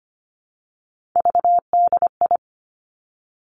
Morse code, heard as a keyed tone, about 25 wpm, 700 Hz. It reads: VDI
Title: Three Challenging Letter-Number Combinations 25wpm